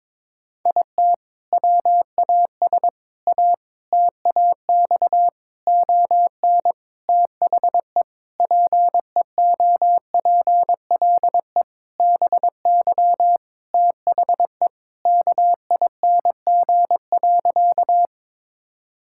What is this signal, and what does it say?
Morse code, heard as a keyed tone, about 21 wpm, 700 Hz. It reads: IT WAS A TAX ON THE PEOPLE BY THE KING.
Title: Sentences from Top 1000 Words - Review Part 5 - 22wpm